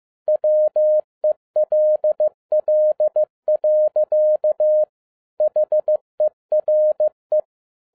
Morse code, heard as a keyed tone, about 15 wpm, 610 Hz. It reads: WELL. HERE